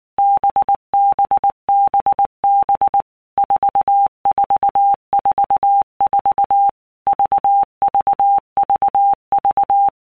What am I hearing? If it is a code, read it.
Morse code, heard as a keyed tone, about 19 wpm, 800 Hz. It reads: BBBB 4444 VVVV
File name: Basic Letter Groups By Blocks / Groups@18 / VB46@18